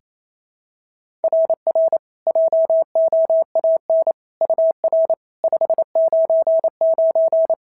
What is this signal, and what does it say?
Morse code, heard as a keyed tone, about 25 wpm, 650 Hz. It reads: RR JOAN UR 599
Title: SOTA QSO Practice 28wpm